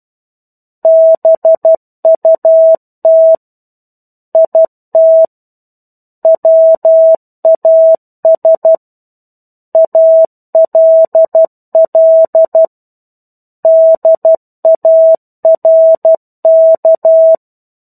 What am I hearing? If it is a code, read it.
Morse code, heard as a keyed tone, about 12 wpm, 650 Hz. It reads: BUT IT WAS ALL DARK
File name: 01-Chapter-01 12 WPM